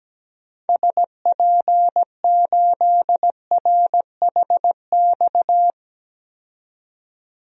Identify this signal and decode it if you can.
Morse code, heard as a keyed tone, about 17 wpm, 700 Hz. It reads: SP8RHX
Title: DXCC-100 Call Signs - 6 Characters - 17wpm